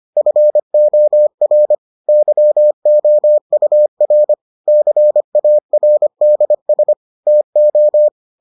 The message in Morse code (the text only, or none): FOR YOUR CARDS TO